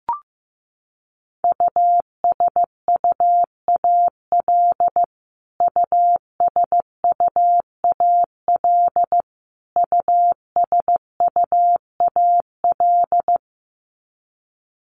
Morse code, heard as a keyed tone, about 15 wpm, 700 Hz. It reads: USUAL USUAL USUAL
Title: Top 500 Words Repeated 15wpm